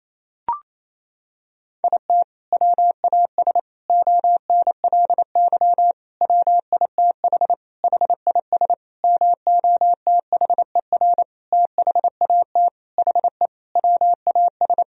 Morse code, heard as a keyed tone, about 28 wpm, 700 Hz. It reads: IT WAS ONLY WITH HIS MOTHER THAT HE WAS
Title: Sentences from Top 1000 Words - Review Part 1 - 28wpm